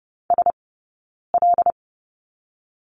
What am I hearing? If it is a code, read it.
Morse code, heard as a keyed tone, about 30 wpm, 700 Hz. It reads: S L
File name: Lesson 17d - Letter u - Two Characters 30wpm